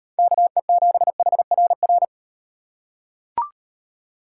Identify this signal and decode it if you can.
Morse code, heard as a keyed tone, about 38 wpm, 700 Hz. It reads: KE7HRR